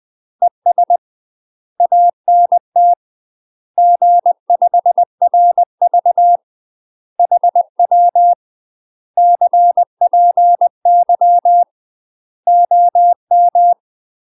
Morse code, heard as a keyed tone, about 20 wpm, 700 Hz. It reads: ES ANT G5RV HW CPY OM
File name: UK MASTER QSOs TRAINING/20WPM